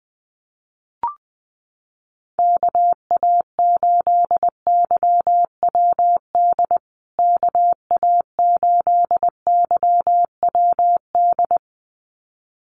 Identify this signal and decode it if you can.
Morse code, heard as a keyed tone, about 20 wpm, 700 Hz. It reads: KA8YWD KA8YWD
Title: USA Call Signs Repeated 2x 20wpm